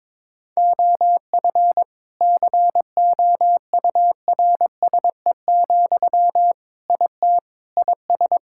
Morse code, heard as a keyed tone, about 22 wpm, 700 Hz. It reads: OF COURSE, IT IS
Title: Sentences from Top 1600 Words 22wpm